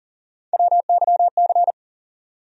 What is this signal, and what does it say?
Morse code, heard as a keyed tone, about 40 wpm, 700 Hz. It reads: WYC
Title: USA Call Sign Postfixes - 3 Letters 40wpm